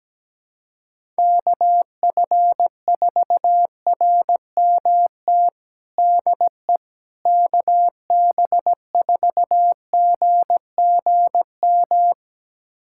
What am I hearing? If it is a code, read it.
Morse code, heard as a keyed tone, about 17 wpm, 700 Hz. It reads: KF4RMT DE KB4GGM
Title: USA Callsign DE Callsign - 6 Characters - 17wpm